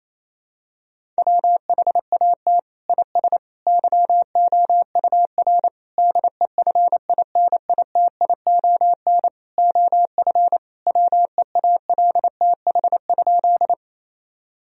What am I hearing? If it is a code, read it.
Morse code, heard as a keyed tone, about 28 wpm, 700 Hz. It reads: WHAT IS YOUR DEFINITION OF WEALTH?
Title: Sentences from Top 1000 Words - Review Part 4 - 28wpm